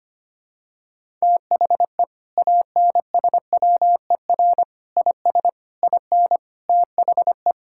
Morse code, heard as a keyed tone, about 25 wpm, 700 Hz. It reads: THE ANSWER IS IN THE
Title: Sentences from Top 1000 Words - Review Part 5 - 25wpm